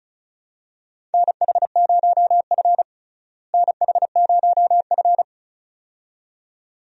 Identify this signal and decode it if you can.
Morse code, heard as a keyed tone, about 36 wpm, 700 Hz. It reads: NH0F NH0F
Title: USA Call Signs - 4 Characters - Repeated 2x - 35wpm